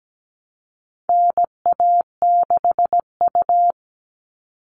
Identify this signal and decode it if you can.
Morse code, heard as a keyed tone, about 17 wpm, 700 Hz. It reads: NA6U
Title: All Cut Numbers - 4-Digit 17wpm